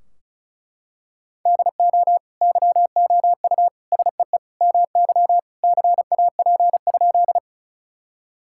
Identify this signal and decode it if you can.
Morse code, heard as a keyed tone, about 36 wpm, 700 Hz. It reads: DO YOU SEE MY CAP?
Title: Sentences from Top 1600 Words 35wpm